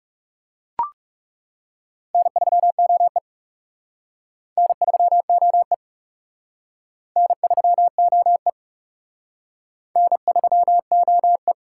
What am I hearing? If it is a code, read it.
Morse code, heard as a keyed tone, about 39 wpm, 700 Hz. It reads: N3OE N3OE N3OE N3OE